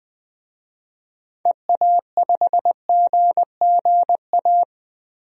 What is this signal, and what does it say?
Morse code, heard as a keyed tone, about 20 wpm, 700 Hz. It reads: EA5GGA
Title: DXCC-100 Call Signs - 6 Characters - Repeated 3x - 20wpm